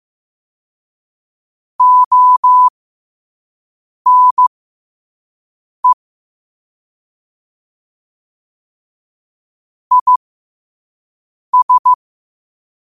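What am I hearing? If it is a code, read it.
Morse code, heard as a keyed tone, about 15 wpm, spaced as for 5 wpm, 1000 Hz. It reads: ONE IS
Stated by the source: Morse code sent at 5 wpm.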